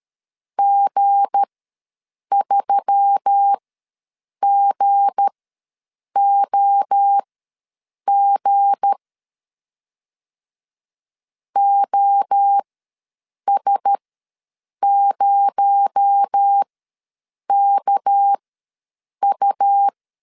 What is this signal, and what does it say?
Morse code, heard as a keyed tone, about 13 wpm, spaced as for 7 wpm, 800 Hz. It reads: G3GOG OS0KU
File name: LEZIONI / 7 GF9Z / 7 G ws 12wpm 3140